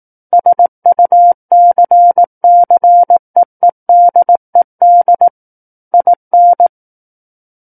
Morse code, heard as a keyed tone, about 18 wpm, 700 Hz. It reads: SUCCEEDED IN
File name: 18 WPM